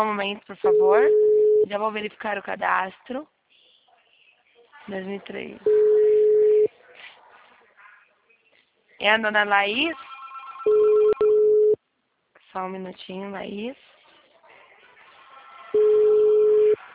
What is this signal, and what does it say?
Ringback tone (Europe, 425 Hz), with speech.